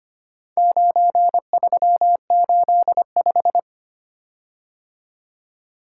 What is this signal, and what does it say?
Morse code, heard as a keyed tone, about 25 wpm, 700 Hz. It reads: 9385
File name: practice / 25wpm / Four Letter-Number Combinations - 25wpm